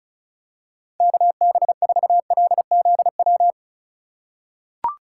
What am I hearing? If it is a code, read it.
Morse code, heard as a keyed tone, about 36 wpm, 700 Hz. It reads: KB4LZW
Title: USA Call Signs - 6 Characters - 3x Character Spacing - 35wpm